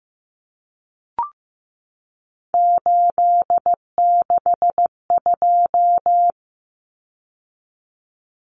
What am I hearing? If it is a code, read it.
Morse code, heard as a keyed tone, about 15 wpm, 700 Hz. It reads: 862
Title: Numbers - 3-Digit 15wpm